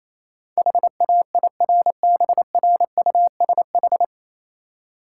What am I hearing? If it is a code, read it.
Morse code, heard as a keyed tone, about 27 wpm, 700 Hz. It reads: HAIRBRUSH
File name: Easy Words - Speed-Racing with Hints - 22wpm